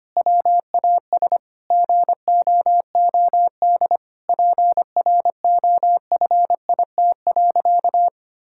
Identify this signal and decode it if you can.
Morse code, heard as a keyed tone, about 25 wpm, 700 Hz. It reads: WAS GOOD PROFIT.
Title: Sentences from Top 1600 Words 25wpm